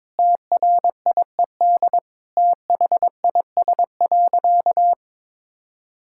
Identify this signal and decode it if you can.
Morse code, heard as a keyed tone, about 22 wpm, 700 Hz. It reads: TRIED THIS.
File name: practice / 22wpm / Sentences with 4-Word Phrases - 22wpm